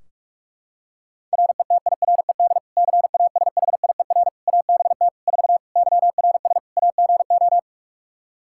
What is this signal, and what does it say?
Morse code, heard as a keyed tone, about 41 wpm, 700 Hz. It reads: RETIRED CASHIER ABT 4 YRS AGO
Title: Practice QSO - Exchange 3 - More Info - 45wpm